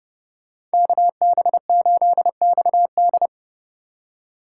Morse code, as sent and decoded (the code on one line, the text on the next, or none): -.- -... ---.. -..- -..
KB8XD